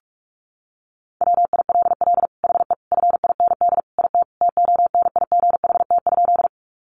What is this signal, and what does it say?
Morse code, heard as a keyed tone, about 44 wpm, 700 Hz. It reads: WILL HE FIND IT TONIGHT?
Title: Sentences from Top 600 Words 45wpm